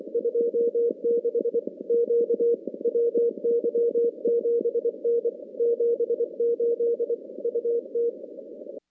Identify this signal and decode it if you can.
Morse code, heard as a keyed tone, about 24 wpm, 470 Hz. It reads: 26 Q WY7N 78 UT
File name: contests / CW-SS-k5tr- -2200